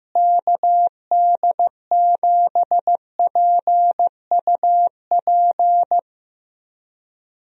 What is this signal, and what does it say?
Morse code, heard as a keyed tone, about 15 wpm, 700 Hz. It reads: KD7PUP
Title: USA Call Signs - 6 Characters - Repeated 2x - 15wpm